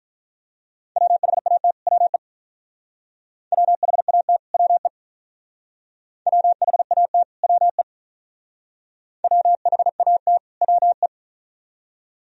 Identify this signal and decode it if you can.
Morse code, heard as a keyed tone, about 42 wpm, 700 Hz. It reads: WHAT WE WHAT WE WHAT WE WHAT WE
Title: Speed-Racing - Sets of 2 Words 35wpm